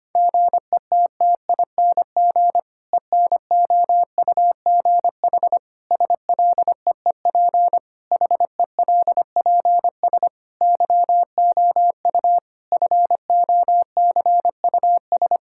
Morse code, heard as a keyed tone, about 25 wpm, 700 Hz. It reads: GETTING ENOUGH SLEEP HELPS YOU FOCUS
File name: practice / 25wpm / Difficult Sentences with Hints - 25wpm